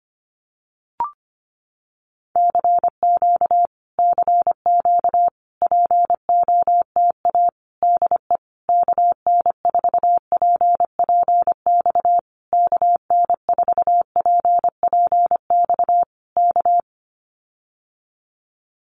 Morse code, heard as a keyed tone, about 25 wpm, 700 Hz. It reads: CQ CQ POTA DE KN4PPX KN4PPX K
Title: Common QSO Phrases Encore 25wpm